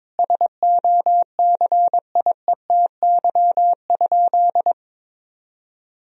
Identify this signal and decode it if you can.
Morse code, heard as a keyed tone, about 22 wpm, 700 Hz. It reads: SOCIETY?
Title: Sentences from Top 1000 Words - Review Part 4 - 22wpm